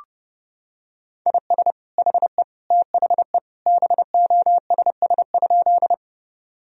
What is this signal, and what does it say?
Morse code, heard as a keyed tone, about 30 wpm, 700 Hz. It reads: IS HE THE BOSS?
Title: Sentences from Top 1000 Words - Review Part 4 - 30wpm